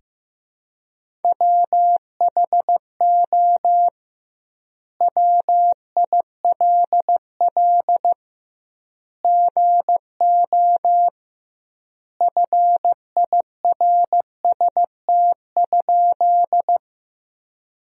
Morse code, heard as a keyed tone, about 16 wpm, 700 Hz. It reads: WHO WILL GO FIRST?
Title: 2x Word-Spacing - Sentences from Top 100 Words - 15wpm